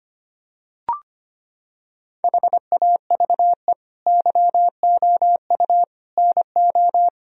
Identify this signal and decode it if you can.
Morse code, heard as a keyed tone, about 25 wpm, 700 Hz. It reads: HAVE YOU NO
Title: Sentences from Top 1000 Words - Review Part 1 - 25wpm